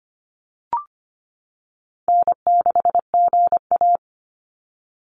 Morse code, heard as a keyed tone, about 25 wpm, 700 Hz. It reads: N6GA
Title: Easy Call Signs - 25wpm